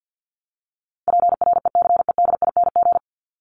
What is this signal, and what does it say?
Morse code, heard as a keyed tone, about 49 wpm, 700 Hz. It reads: PRECEDING
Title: ING Suffix - Top 100 Words with Base Words 50wpm